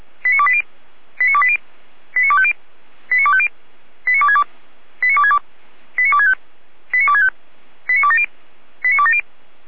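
Five-tone selective call (ZVEI), heard as quick runs of five tones, tones of about 70 ms. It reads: 89280 89280 89250 89250 89262 89262 89265 89265 89270 89270
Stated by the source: Five-tone ZVEI selective call 89246, received on an SDR.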